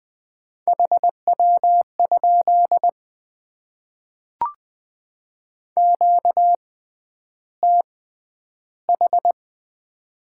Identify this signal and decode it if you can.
Morse code, heard as a keyed tone, about 20 wpm, spaced as for 10 wpm, 700 Hz. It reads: HW? QTH
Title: Top 20 QSO Elements - 6x Character Spacing - 20wpm